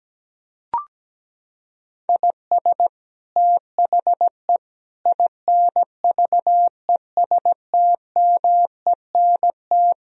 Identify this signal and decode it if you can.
Morse code, heard as a keyed tone, about 17 wpm, 700 Hz. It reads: IS THE INVESTMENT
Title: Sentences from Top 1000 Words - Review Part 4 - 17wpm